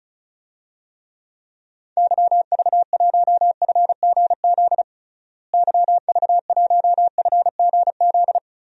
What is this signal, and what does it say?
Morse code, heard as a keyed tone, about 36 wpm, 700 Hz. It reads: YV1FGZ YV1FGZ
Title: DXCC-100 Call Signs - 6 Characters - Repeated 3x - 35wpm